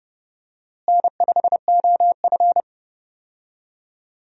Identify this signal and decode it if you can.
Morse code, heard as a keyed tone, about 30 wpm, 700 Hz. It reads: N5OF
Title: USA Call Signs - 4 Characters - Repeated 3x - 30wpm